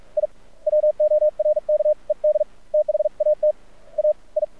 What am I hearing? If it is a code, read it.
Morse code, heard as a keyed tone, about 48 wpm, spaced as for 38 wpm, 610 Hz. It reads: I WORKED THAT AI
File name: archived / CRSnet